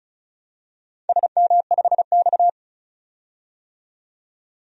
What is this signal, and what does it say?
Morse code, heard as a keyed tone, about 34 wpm, 700 Hz. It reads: SM5X
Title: DXCC-100 Call Signs - 4 Characters - 35wpm